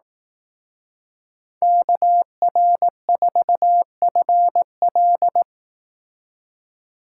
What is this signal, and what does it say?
Morse code, heard as a keyed tone, about 18 wpm, 700 Hz. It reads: KR4FL
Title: Speed Racing - USA 2x2 Call Signs 15wpm